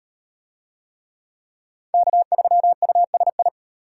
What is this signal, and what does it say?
Morse code, heard as a keyed tone, about 37 wpm, 700 Hz. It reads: K3USI